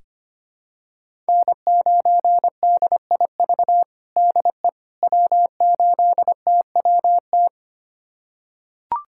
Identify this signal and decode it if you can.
Morse code, heard as a keyed tone, about 25 wpm, 700 Hz. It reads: N9DIV DE W8TWT
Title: USA Callsign DE Callsign - 5 Characters - 25wpm